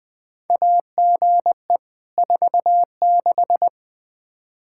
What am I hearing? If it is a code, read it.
Morse code, heard as a keyed tone, about 20 wpm, 700 Hz. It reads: AGE 46
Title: Practice QSO - Exchange 3 - More Info - 20wpm